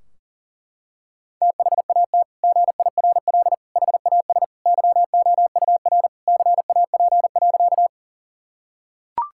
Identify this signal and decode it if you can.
Morse code, heard as a keyed tone, about 40 wpm, 700 Hz. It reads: THAT GIRL HAS YOUR CAP.
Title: Sentences from Top 1600 Words 40wpm